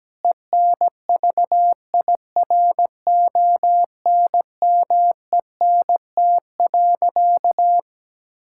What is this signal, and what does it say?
Morse code, heard as a keyed tone, about 17 wpm, 700 Hz. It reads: ENVIRONMENT.
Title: Sentences from Top 1000 Words - Review Part 6 - 17wpm